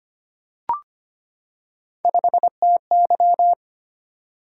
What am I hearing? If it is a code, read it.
Morse code, heard as a keyed tone, about 25 wpm, 700 Hz. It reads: 5TY
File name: Three Letter-Number Combinations - 25wpm